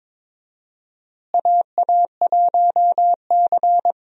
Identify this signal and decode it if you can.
Morse code, heard as a keyed tone, about 21 wpm, 700 Hz. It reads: AA1C
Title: USA Call Signs - 4 Characters - Repeated 2x - 22wpm